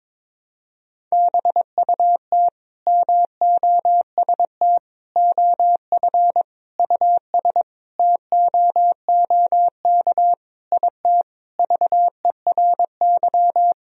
Morse code, heard as a keyed tone, about 22 wpm, 700 Hz. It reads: BUT MOST OF US TOOK IT VERY
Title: Sentences from Top 1000 Words - Review Part 1 - 22wpm